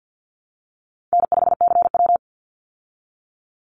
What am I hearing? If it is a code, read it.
Morse code, heard as a keyed tone, about 49 wpm, 700 Hz. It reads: N5CW
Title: Easy Call Signs Repeated 3x - 50wpm